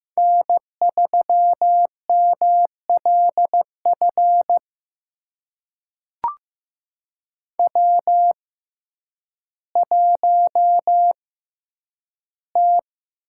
Morse code, heard as a keyed tone, about 15 wpm, spaced as for 9 wpm, 700 Hz. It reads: N3MLF W1T
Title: USA Call Signs - 6x Character Spacing - 15wpm